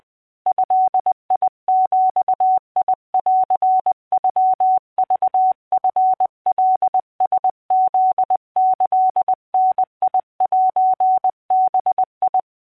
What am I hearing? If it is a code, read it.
Morse code (Japanese Wabun code), heard as a keyed tone, about 20 wpm, 750 Hz. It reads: ドビンノクチカラフキダセバ